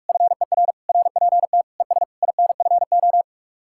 Morse code, heard as a keyed tone, about 62 wpm, spaced as for 53 wpm, 700 Hz. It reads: FER RPT ES INFO
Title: Speed-Racing Common QSO Phrases - 45wpm